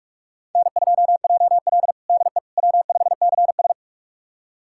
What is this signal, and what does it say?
Morse code, heard as a keyed tone, about 46 wpm, 700 Hz. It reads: N2JL DE W5CS
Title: USA Callsign DE Callsign - 4 Characters - 45wpm